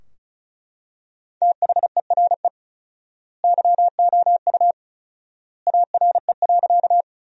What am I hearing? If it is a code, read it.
Morse code, heard as a keyed tone, about 36 wpm, 700 Hz. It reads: THERE YOU ARE.
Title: 4x Word-Spacing - Sentences from Top 100 Words - 35wpm